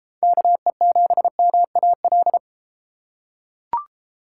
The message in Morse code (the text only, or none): KE7MAL